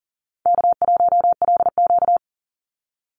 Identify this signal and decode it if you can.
Morse code, heard as a keyed tone, about 40 wpm, 700 Hz. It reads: K1LQ
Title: USA Call Signs 40wpm